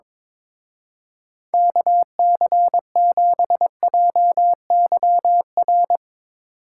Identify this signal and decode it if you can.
Morse code, heard as a keyed tone, about 22 wpm, 700 Hz. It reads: KC7JYR
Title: Speed Racing - USA 2x3 Call Signs 22wpm